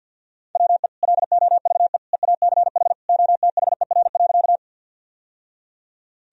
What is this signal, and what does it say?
Morse code, heard as a keyed tone, about 52 wpm, 700 Hz. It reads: WE LOVE EACH OTHER.
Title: Sentences from Top 1000 Words - Review Part 1 - 50wpm